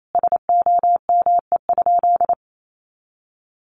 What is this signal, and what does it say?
Morse code, heard as a keyed tone, about 28 wpm, 700 Hz. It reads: SOME?
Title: Sentences from Top 100 Words 28wpm